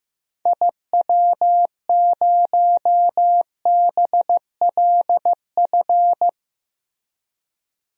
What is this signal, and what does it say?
Morse code, heard as a keyed tone, about 15 wpm, 700 Hz. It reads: IW0BLF
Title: DXCC-100 Call Signs - 6 Characters - Repeated 3x - 15wpm